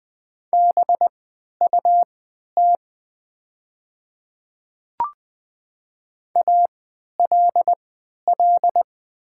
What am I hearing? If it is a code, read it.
Morse code, heard as a keyed tone, about 20 wpm, spaced as for 11 wpm, 700 Hz. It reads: BUT ALL
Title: 3x Character Spacing - Top 200 Words - 20wpm